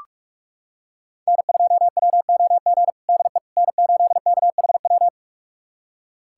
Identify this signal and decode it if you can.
Morse code, heard as a keyed tone, about 46 wpm, 700 Hz. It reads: N2WOG DE N8KHW
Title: USA Callsign DE Callsign - 5 Characters - 45wpm